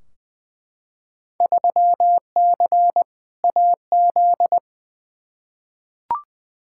Morse code, heard as a keyed tone, about 20 wpm, 700 Hz. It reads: 3C AZ
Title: Field Day Exchanges Encore - 20wpm